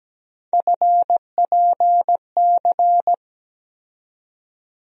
Morse code, heard as a keyed tone, about 17 wpm, 700 Hz. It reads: FPC
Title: USA Call Sign Postfixes - 3 Letters 17wpm